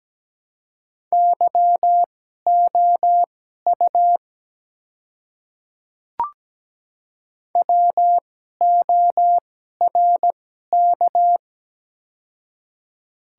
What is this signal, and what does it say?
Morse code, heard as a keyed tone, about 17 wpm, spaced as for 12 wpm, 700 Hz. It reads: YOU WORK
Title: 2x Character Spacing - Top 200 Words - 17wpm